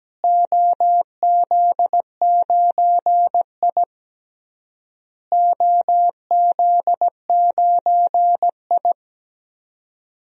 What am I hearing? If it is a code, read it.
Morse code, heard as a keyed tone, about 17 wpm, 700 Hz. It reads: OZ9I OZ9I
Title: DXCC-100 Call Signs - 4 Characters - Repeated 3x - 17wpm